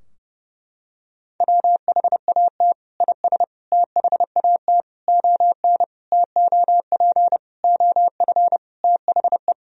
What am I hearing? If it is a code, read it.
Morse code, heard as a keyed tone, about 30 wpm, 700 Hz. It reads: WHAT IS THAT ON TOP OF THE
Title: Sentences from Top 1000 Words - Review Part 2 - 30wpm